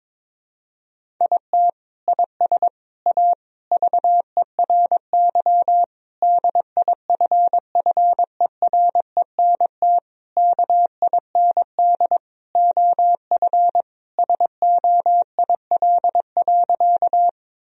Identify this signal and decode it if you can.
Morse code, heard as a keyed tone, about 22 wpm, 700 Hz. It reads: IT IS A VERY DIFFERENT KIND OF SOIL.
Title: Sentences from Top 1000 Words - Review Part 4 - 22wpm